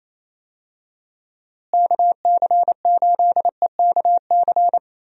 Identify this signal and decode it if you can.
Morse code, heard as a keyed tone, about 28 wpm, 700 Hz. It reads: KC8EKC
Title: USA Call Signs - 6 Characters - Repeated 2x - 28wpm